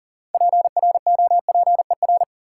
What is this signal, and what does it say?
Morse code, heard as a keyed tone, about 40 wpm, 700 Hz. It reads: PROPER
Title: Sentences from Top 800 Words - New Word Drill - 40wpm